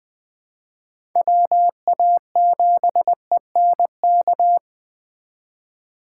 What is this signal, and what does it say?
Morse code, heard as a keyed tone, about 20 wpm, 700 Hz. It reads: WA7ENK
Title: Speed Racing - USA 2x3 Call Signs 17wpm